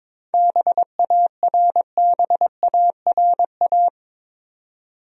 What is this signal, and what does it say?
Morse code, heard as a keyed tone, about 22 wpm, 700 Hz. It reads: BARBARA
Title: USA Common Names - 22wpm